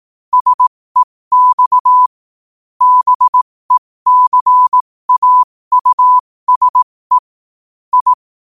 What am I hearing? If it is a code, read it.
Morse code, heard as a keyed tone, about 18 wpm, spaced as for 15 wpm, 1000 Hz. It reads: SEX BECAUSE I
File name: qotdfiles / 15wpm